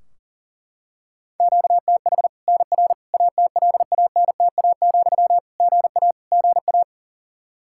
Morse code, heard as a keyed tone, about 40 wpm, 700 Hz. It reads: QTH NR ATLANTA, GA GA
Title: Practice QSO - Exchange 1 - Beginning - 40wpm